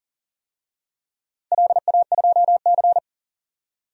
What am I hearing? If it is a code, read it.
Morse code, heard as a keyed tone, about 40 wpm, 700 Hz. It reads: LA2C
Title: DXCC-100 Call Signs - 4 Characters - 40wpm